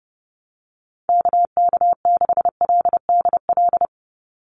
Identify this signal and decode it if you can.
Morse code, heard as a keyed tone, about 30 wpm, 700 Hz. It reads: KK6LDL
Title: USA Call Signs 30wpm